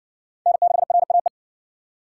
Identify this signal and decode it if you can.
Morse code, heard as a keyed tone, about 60 wpm, 700 Hz. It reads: N6RAE